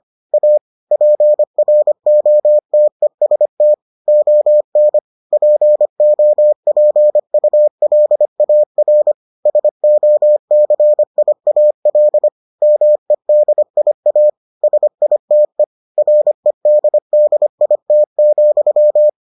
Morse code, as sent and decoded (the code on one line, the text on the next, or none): .- / .--. .-. --- - . ... - / --- -. / .--. --- .--. ..- .-.. .- .-. / ... --- -.-. .. .- .-.. / -- . -.. .. .- / ... .. - . / .-. -.. -.. .. - --..--
A PROTEST ON POPULAR SOCIAL MEDIA SITE REDDIT,